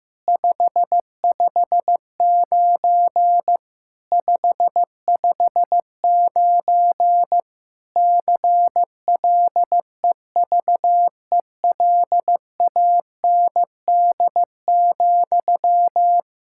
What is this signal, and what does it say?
Morse code, heard as a keyed tone, about 15 wpm, 700 Hz. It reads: 559 559 CLEVELAND,
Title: K1USN SST QSO Practice 15wpm